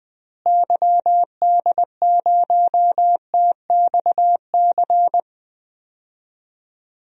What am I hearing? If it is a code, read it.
Morse code, heard as a keyed tone, about 20 wpm, 700 Hz. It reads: YD0TXC
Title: DXCC-100 Call Signs - 6 Characters - Repeated 3x - 20wpm